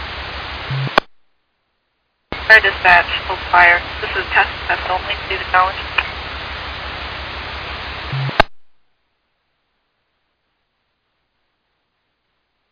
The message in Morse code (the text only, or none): none